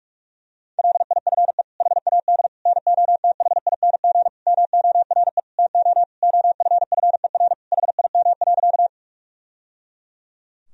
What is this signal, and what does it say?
Morse code, heard as a keyed tone, about 46 wpm, 700 Hz. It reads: LIFE HAD NOTHING MORE TO OFFER HIM.
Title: Sentences from Top 1000 Words - Review Part 4 - 45wpm